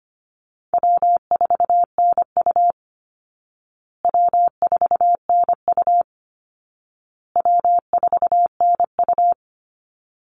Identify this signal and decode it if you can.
Morse code, heard as a keyed tone, about 25 wpm, 700 Hz. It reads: W4NU W4NU W4NU
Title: Easy Call Signs Repeated 3x - 25wpm